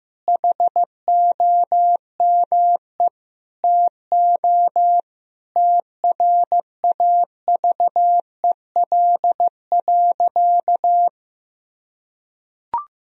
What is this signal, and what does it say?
Morse code, heard as a keyed tone, about 15 wpm, 700 Hz. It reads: HOME TO TRAVEL.
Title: Sentences from Top 1000 Words - Review Part 2 - 15wpm